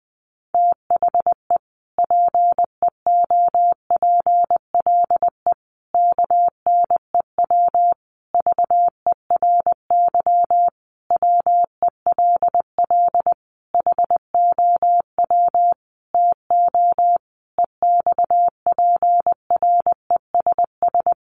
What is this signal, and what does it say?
Morse code, heard as a keyed tone, about 20 wpm, 700 Hz. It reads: THE PEOPLE KNEW VERY WELL HOW TO EXPRESS
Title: Sentences from Top 1100 Words 20wpm